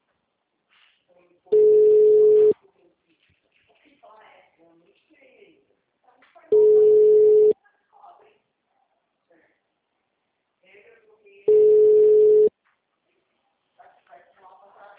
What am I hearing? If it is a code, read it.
Ringback tone (Europe, 425 Hz), with speech.
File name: out- -2001- - - .20553